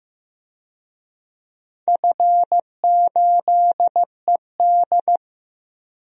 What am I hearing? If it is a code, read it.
Morse code, heard as a keyed tone, about 15 wpm, 700 Hz. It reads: F8ED
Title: DXCC-100 Call Signs - 4 Characters - 15wpm